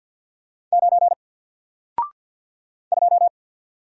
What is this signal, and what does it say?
Morse code, heard as a keyed tone, about 48 wpm, 700 Hz. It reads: 9 2
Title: Single Letter-Number Alt 50wpm